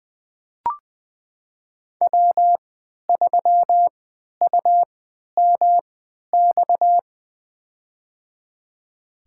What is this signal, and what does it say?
Morse code, heard as a keyed tone, about 20 wpm, spaced as for 11 wpm, 700 Hz. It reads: W3UMX
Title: USA Call Signs - 5 Characters - 3x Character Spacing - 20wpm